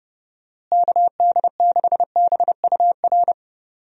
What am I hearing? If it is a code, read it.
Morse code, heard as a keyed tone, about 32 wpm, 700 Hz. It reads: KD6BUR
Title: USA Call Signs - 6 Characters - Repeated 2x - 30wpm